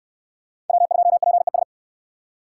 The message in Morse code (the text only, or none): W2PES